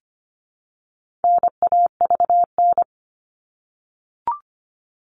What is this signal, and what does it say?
Morse code, heard as a keyed tone, about 25 wpm, 700 Hz. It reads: NAVN
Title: All Cut Numbers - 4-Digit 25wpm